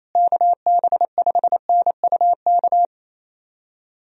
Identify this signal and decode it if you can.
Morse code, heard as a keyed tone, about 28 wpm, 700 Hz. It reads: KB5NUK
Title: USA Call Signs - 6 Characters - Repeated 2x - 28wpm